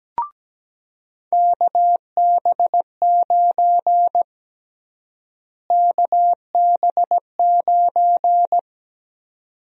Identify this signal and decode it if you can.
Morse code, heard as a keyed tone, about 17 wpm, 700 Hz. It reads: KB9 KB9
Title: USA Call Sign Prefixes - Repeated 2x - 17wpm